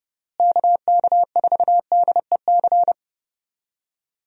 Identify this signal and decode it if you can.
Morse code, heard as a keyed tone, about 31 wpm, 700 Hz. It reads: KK4DEC
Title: USA Call Signs - 6 Characters - Repeated 2x - 30wpm